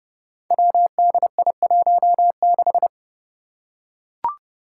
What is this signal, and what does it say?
Morse code, heard as a keyed tone, about 29 wpm, 700 Hz. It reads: WDI16
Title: Five Challenging Letter-Number Combinations 30wpm